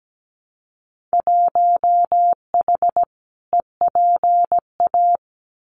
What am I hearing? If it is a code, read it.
Morse code, heard as a keyed tone, about 17 wpm, 700 Hz. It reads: 1H EPA
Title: Winter Field Day Exchanges Encore 17wpm